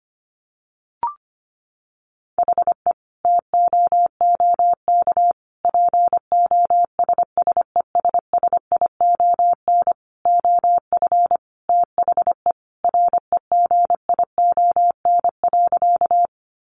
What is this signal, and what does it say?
Morse code, heard as a keyed tone, about 25 wpm, 700 Hz. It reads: HE TOOK POSSESSION OF THE REGION.
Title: Sentences from Top 1500 Words 25wpm